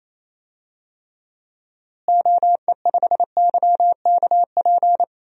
Morse code, heard as a keyed tone, about 28 wpm, 700 Hz. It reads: OE5YKP
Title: DXCC-100 Call Signs - 6 Characters - Repeated 2x - 28wpm